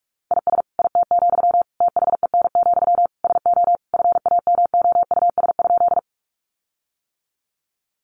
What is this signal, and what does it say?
Morse code, heard as a keyed tone, about 44 wpm, 700 Hz. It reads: IS IT, THEN, SO FAMOUS?
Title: Sentences from Top 1500 Words 45wpm